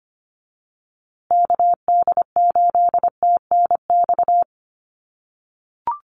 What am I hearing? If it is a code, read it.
Morse code, heard as a keyed tone, about 25 wpm, 700 Hz. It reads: KD8TNX
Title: USA Call Signs Repeated 3x 25wpm